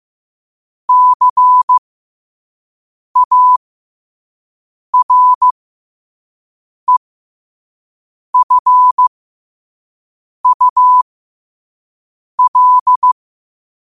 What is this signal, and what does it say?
Morse code, heard as a keyed tone, about 15 wpm, spaced as for 5 wpm, 1000 Hz. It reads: CAREFUL
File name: qotdfiles / 5wpm